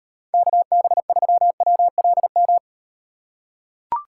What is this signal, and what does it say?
Morse code, heard as a keyed tone, about 38 wpm, 700 Hz. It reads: KB3WLM